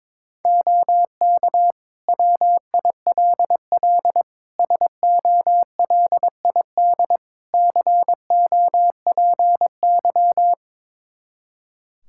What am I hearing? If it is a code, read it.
Morse code, heard as a keyed tone, about 23 wpm, 700 Hz. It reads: OK WILL SOLID COPY
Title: Practice QSO - Exchange 3 - More Info - 22wpm